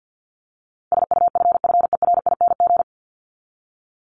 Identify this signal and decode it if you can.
Morse code, heard as a keyed tone, about 49 wpm, 700 Hz. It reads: SUFFERING
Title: ING Suffix - Top 100 Words with Base Words 50wpm